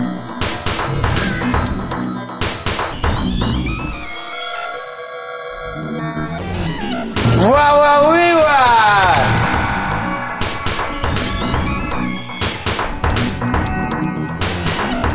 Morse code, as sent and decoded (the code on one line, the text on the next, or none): none